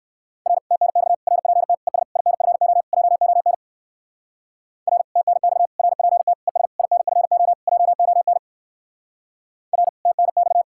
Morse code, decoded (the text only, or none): R TNX RPT ES INFO JON R TNX RPT ES INFO JON R TNX